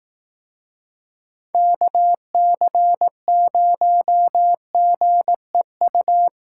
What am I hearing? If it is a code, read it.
Morse code, heard as a keyed tone, about 18 wpm, 700 Hz. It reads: KC0GEU
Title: Speed Racing - USA Call Signs 15wpm